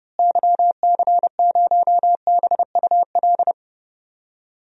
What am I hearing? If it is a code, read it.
Morse code, heard as a keyed tone, about 30 wpm, 700 Hz. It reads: YC0BUL
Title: DXCC-100 Call Signs - 6 Characters - Repeated 3x - 30wpm